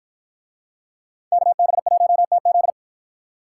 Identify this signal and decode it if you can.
Morse code, heard as a keyed tone, about 51 wpm, 700 Hz. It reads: KB1TZ